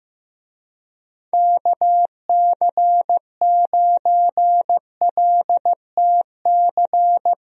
Morse code, heard as a keyed tone, about 16 wpm, 700 Hz. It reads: KC9LTC
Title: USA Call Signs - 6 Characters - Repeated 2x - 15wpm